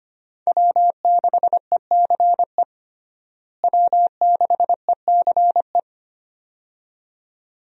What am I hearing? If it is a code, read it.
Morse code, heard as a keyed tone, about 25 wpm, 700 Hz. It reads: W6ECE W6ECE
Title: USA Call Signs - 5 Characters - Repeated 2x - 25wpm